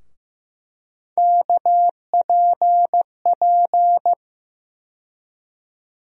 Morse code, heard as a keyed tone, about 15 wpm, 700 Hz. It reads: KPP
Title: USA Call Sign Postfixes - 3 Letters 15wpm